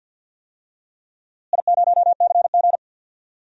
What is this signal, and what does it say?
Morse code, heard as a keyed tone, about 52 wpm, 700 Hz. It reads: I0CG
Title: DXCC-100 Call Signs - 4 Characters - Repeated 2x - 50wpm